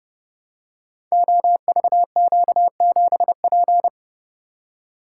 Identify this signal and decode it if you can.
Morse code, heard as a keyed tone, about 31 wpm, 700 Hz. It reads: OVQ7P
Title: Code Groups - 5 Letter:Numbers - 30wpm